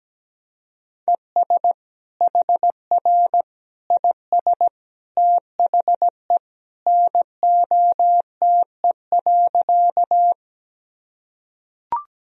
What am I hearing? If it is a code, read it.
Morse code, heard as a keyed tone, about 17 wpm, 700 Hz. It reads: ES HR IS THE NOTE.
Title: Sentences from Top 500 Words and Abbreviations 17wpm